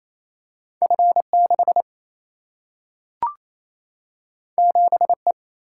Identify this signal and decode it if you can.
Morse code, heard as a keyed tone, about 28 wpm, 700 Hz. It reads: F6 7E
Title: Single-Letter Single-Number 28wpm